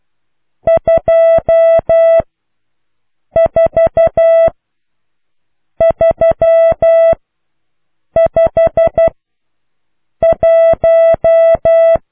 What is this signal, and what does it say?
Morse code, heard as a keyed tone, about 12 wpm, spaced as for 6 wpm, 660 Hz. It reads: 24351